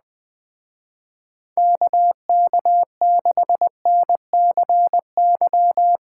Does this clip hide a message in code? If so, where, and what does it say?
Yes: Morse code from 1.6 to 6.0 s, heard as a keyed tone, about 20 wpm, 700 Hz. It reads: KK6NCY